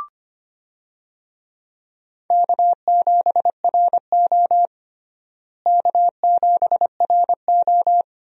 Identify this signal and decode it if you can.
Morse code, heard as a keyed tone, about 25 wpm, 700 Hz. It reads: K7RO K7RO
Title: USA Call Signs - 4 Characters - Repeated 2x - 25wpm